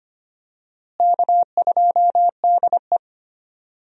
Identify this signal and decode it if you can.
Morse code, heard as a keyed tone, about 25 wpm, 700 Hz. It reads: K2DE
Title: USA Call Signs - 4 Characters - Repeated 2x - 25wpm